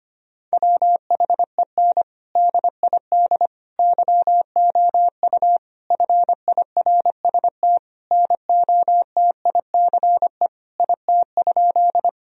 Morse code, heard as a keyed tone, about 25 wpm, 700 Hz. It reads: WHEN DID YOU FIRST NOTICE IT?
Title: Sentences from Top 1000 Words - Review Part 2 - 25wpm